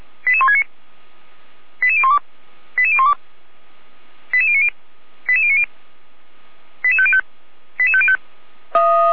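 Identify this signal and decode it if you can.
Five-tone selective call (ZVEI), heard as quick runs of five tones, tones of about 70 ms. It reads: 88169 88012 88012 88090 88090 88505 88505
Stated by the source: ZVEI five-tone call 88191 received by SDR.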